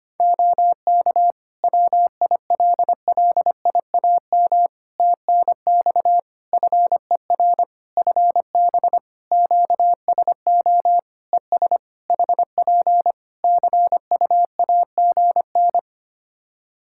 Morse code, heard as a keyed tone, about 25 wpm, 700 Hz. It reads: OK WILLIAM TNX FER FB QSO ES HP CUAGN
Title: Practice QSO - Exchange 4 - Ending - 25wpm